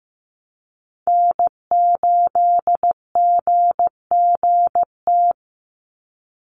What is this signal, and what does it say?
Morse code, heard as a keyed tone, about 15 wpm, 700 Hz. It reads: N8GGT
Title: USA Call Signs 15wpm